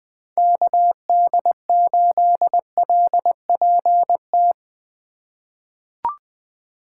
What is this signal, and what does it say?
Morse code, heard as a keyed tone, about 21 wpm, 700 Hz. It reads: KD8LPT
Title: USA Call Signs - 4x Character Spacing - 20wpm